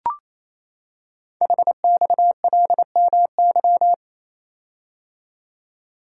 Morse code, heard as a keyed tone, about 27 wpm, 700 Hz. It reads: HXLMY
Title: Code Groups - 5 Letters - 28wpm